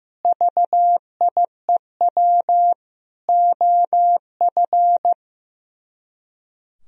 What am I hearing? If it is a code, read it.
Morse code, heard as a keyed tone, about 15 wpm, 700 Hz. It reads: VIEW OF
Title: Speed-Racing - Sets of 5 Words 15wpm